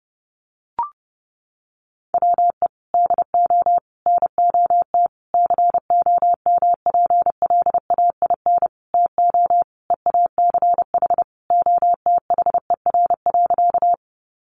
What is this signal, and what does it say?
Morse code, heard as a keyed tone, about 30 wpm, 700 Hz. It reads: WE DO NOT COMPLAIN TO EACH OTHER.
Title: Sentences from Top 1300 Words 30wpm